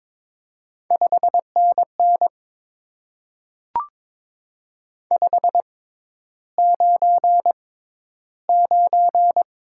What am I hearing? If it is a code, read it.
Morse code, heard as a keyed tone, about 22 wpm, spaced as for 11 wpm, 700 Hz. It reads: 5NN 599
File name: Top 90 QSO Elements - 6x Character Spacing - 22wpm